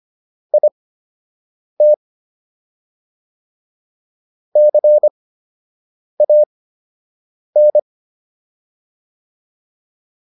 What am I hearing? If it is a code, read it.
Morse code, heard as a keyed tone, about 25 wpm, spaced as for 7 wpm, 600 Hz. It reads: IT CAN